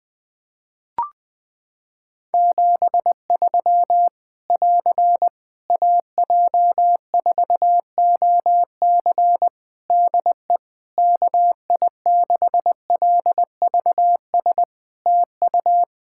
Morse code, heard as a keyed tone, about 20 wpm, 700 Hz. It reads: 73 <AR> AJ4OC DE KI6LVS TU
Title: Practice QSO - Exchange 4 - Ending - 20wpm